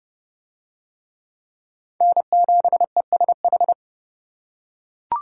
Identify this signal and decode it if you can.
Morse code, heard as a keyed tone, about 29 wpm, 700 Hz. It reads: N7ESH